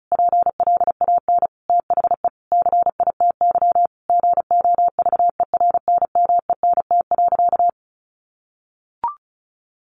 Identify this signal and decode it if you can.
Morse code, heard as a keyed tone, about 34 wpm, 700 Hz. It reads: PLAN THE CITY GOVERNMENT.
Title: Sentences from Top 1400 Words 35wpm